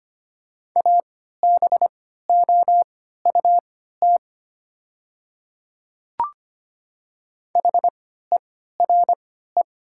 Morse code, heard as a keyed tone, about 25 wpm, spaced as for 14 wpm, 690 Hz. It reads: ABOUT HERE
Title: 3x Character Spacing - Top 300 Words - 25wpm